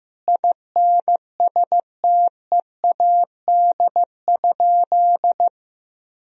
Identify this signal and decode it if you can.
Morse code, heard as a keyed tone, about 15 wpm, 700 Hz. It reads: INSTEAD?
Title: Sentences from Top 1000 Words - Review Part 4 - 15wpm